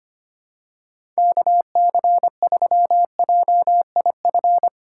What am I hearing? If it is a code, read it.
Morse code, heard as a keyed tone, about 25 wpm, 700 Hz. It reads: KC3JIF
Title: USA Call Signs - 6 Characters - 2x Character Spacing - 25wpm